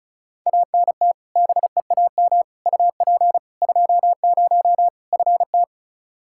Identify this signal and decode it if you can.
Morse code, heard as a keyed tone, about 34 wpm, 700 Hz. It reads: ANT BEAM UP 20 FT
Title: K1USN SST QSO Practice 35wpm